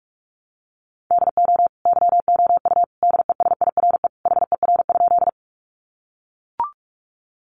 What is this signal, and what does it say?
Morse code, heard as a keyed tone, about 44 wpm, 700 Hz. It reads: DO YOU DESIRE HER?
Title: Sentences from Top 1100 Words 45wpm